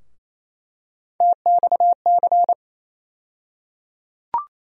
Morse code, heard as a keyed tone, about 28 wpm, 700 Hz. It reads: TXC
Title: USA Call Sign Postfixes - 3 Letters 28wpm